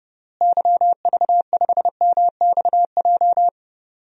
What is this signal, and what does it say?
Morse code, heard as a keyed tone, about 30 wpm, 700 Hz. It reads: YV5MXJ
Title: DXCC-100 Call Signs - 6 Characters - Repeated 2x - 30wpm